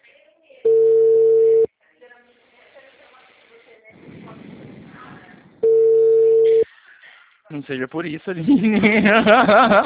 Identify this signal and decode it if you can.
Ringback tone (China, 450 Hz), with speech.
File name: out- -2022- - - .39865